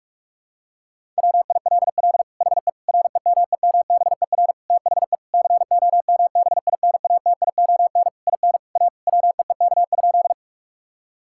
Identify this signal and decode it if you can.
Morse code, heard as a keyed tone, about 46 wpm, 700 Hz. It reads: WILL HE REMEMBER THE COMBINATION IN A WEEK?
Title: Sentences from Top 1000 Words - Review Part 6 - 45wpm